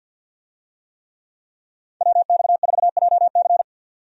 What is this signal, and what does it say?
Morse code, heard as a keyed tone, about 50 wpm, 700 Hz. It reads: WX4JC
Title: USA Call Signs - 5 Characters - 50wpm